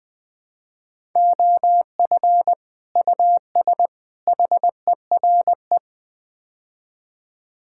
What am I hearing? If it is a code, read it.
Morse code, heard as a keyed tone, about 20 wpm, 700 Hz. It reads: OF US HERE
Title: Build A Phrase - 5-Word Phrases - 20wpm